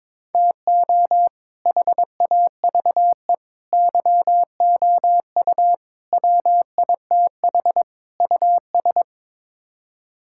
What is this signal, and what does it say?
Morse code, heard as a keyed tone, about 22 wpm, 700 Hz. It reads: TO HAVE YOU WITH US
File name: practice / 22wpm / Build A Phrase - 5-Word Phrases - 22wpm